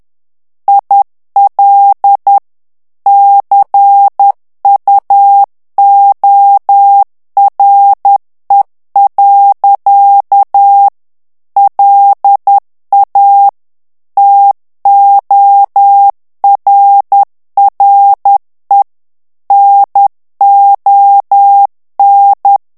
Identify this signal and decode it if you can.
Morse code, heard as a keyed tone, about 10 wpm, 800 Hz. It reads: IL CUORE. LA TORRE NON
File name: morse mp3 / ITALIEN texte clair 1 vitesse 10 mots minute